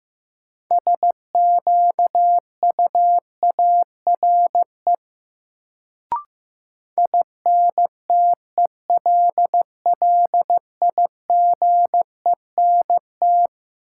Morse code, heard as a keyed tone, about 16 wpm, 700 Hz. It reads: SQUARE INTELLIGENT
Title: Sentences from Top 1000 Words - Review Part 7 - New Word Drill - 15wpm